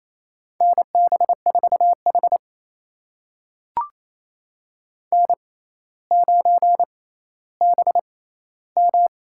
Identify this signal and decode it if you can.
Morse code, heard as a keyed tone, about 29 wpm, spaced as for 14 wpm, 700 Hz. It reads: NB4H N9BM
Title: USA Call Signs - 4 Characters - 6x Character Spacing - 28wpm